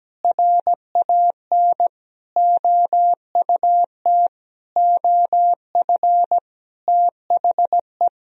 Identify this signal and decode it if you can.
Morse code, heard as a keyed tone, about 17 wpm, 700 Hz. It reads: RAN OUT OF THE
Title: Sentences from Top 500 Words and Abbreviations 17wpm